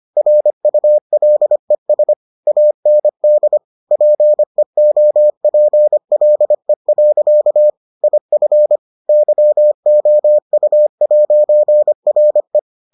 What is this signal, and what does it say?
Morse code, heard as a keyed tone, about 25 wpm, 600 Hz. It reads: RULES AND PEOPLE. IF YOU'RE